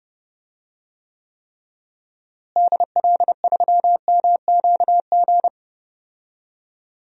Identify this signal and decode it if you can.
Morse code, heard as a keyed tone, about 30 wpm, 700 Hz. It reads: DL3MQG
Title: DXCC-100 Call Signs - 6 Characters - Repeated 3x - 30wpm